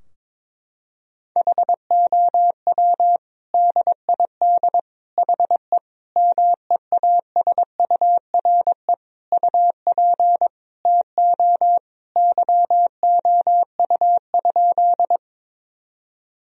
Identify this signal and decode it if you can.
Morse code, heard as a keyed tone, about 22 wpm, 700 Hz. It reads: HOW DID HE MEASURE UP TO YOU?
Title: Sentences from Top 1000 Words - Review Part 2 - 22wpm